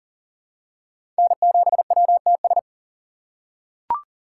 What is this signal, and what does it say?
Morse code, heard as a keyed tone, about 40 wpm, 700 Hz. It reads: N7WTS
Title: USA Call Signs - 4x Character Spacing - 40wpm